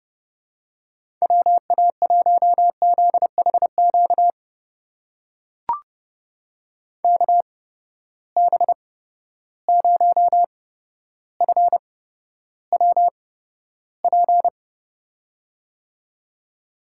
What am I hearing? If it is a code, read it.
Morse code, heard as a keyed tone, about 32 wpm, spaced as for 13 wpm, 700 Hz. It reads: WA1ZHQ KB0FWP
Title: USA Call Signs - 6 Characters - 8x Character Spacing - 30wpm